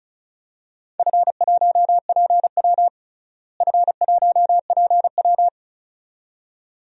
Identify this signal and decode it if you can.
Morse code, heard as a keyed tone, about 34 wpm, 700 Hz. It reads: F1PW F1PW
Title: DXCC-100 Call Signs - 4 Characters - Repeated 2x - 35wpm